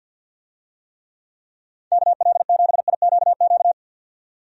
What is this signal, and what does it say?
Morse code, heard as a keyed tone, about 52 wpm, 700 Hz. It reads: KL7IQQ
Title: DXCC-100 Call Signs - 6 Characters - 50wpm